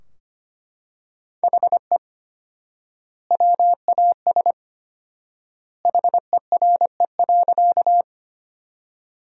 Morse code, heard as a keyed tone, about 25 wpm, 700 Hz. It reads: HE WAS HERE.
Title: 4x Word-Spacing - Sentences from Top 200 Words - 25wpm